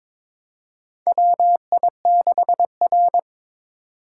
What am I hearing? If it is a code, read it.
Morse code, heard as a keyed tone, about 21 wpm, 700 Hz. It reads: WI6R
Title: USA Call Signs - 4 Characters - 3x Character Spacing - 22wpm